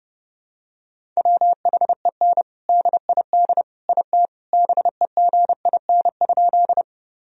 Morse code, heard as a keyed tone, about 30 wpm, 700 Hz. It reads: WHEN DID IT BEGIN?
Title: Sentences from Top 1000 Words - Review Part 1 - 30wpm